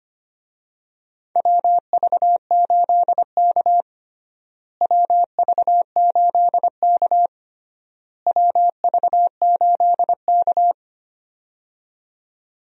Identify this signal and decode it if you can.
Morse code, heard as a keyed tone, about 25 wpm, 700 Hz. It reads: WV8K WV8K WV8K
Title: USA Call Signs - 4 Characters - Repeated 3x - 25wpm